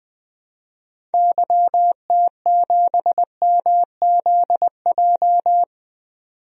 Morse code, heard as a keyed tone, about 20 wpm, 700 Hz. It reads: YT7MZJ
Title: DXCC-100 Call Signs - 6 Characters - 20wpm